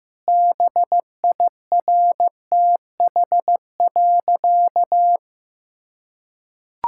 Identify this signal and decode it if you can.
Morse code, heard as a keyed tone, about 16 wpm, 700 Hz. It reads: BIRTH.
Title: Sentences from Top 1000 Words - Review Part 5 - 15wpm